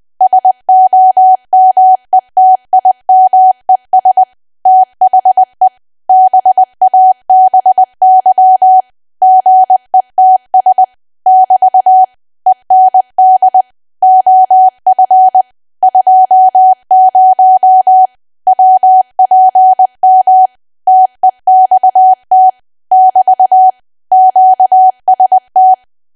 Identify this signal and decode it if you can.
Morse code, heard as a keyed tone, about 20 wpm, 750 Hz. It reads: SOMETIMES THE BABY GETS <BT> END OF 20 WPM TEXT <BT> QST
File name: cwa / W1AW-Level3 / 20WPM